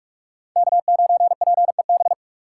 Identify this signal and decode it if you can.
Morse code, heard as a keyed tone, about 41 wpm, 700 Hz. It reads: K9PEB
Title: USA Call Signs - 5 Characters - 3x Character Spacing - 45wpm